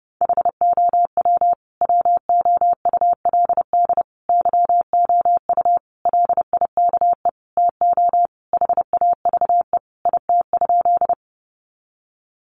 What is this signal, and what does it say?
Morse code, heard as a keyed tone, about 30 wpm, 700 Hz. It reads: HOW WOULD YOU LIKE TO HAVE IT?
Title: Sentences from Top 100 Words 30wpm